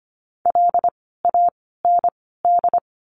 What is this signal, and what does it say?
Morse code, heard as a keyed tone, about 25 wpm, 700 Hz. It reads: L A N D
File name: course / 25:10wpm / 094 Lesson 16f - Letter c - Words 25wpm